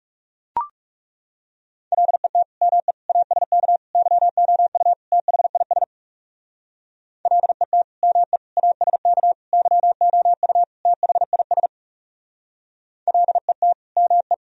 Morse code, heard as a keyed tone, about 39 wpm, 700 Hz. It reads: LET ME ASK YOU THIS LET ME ASK YOU THIS LET ME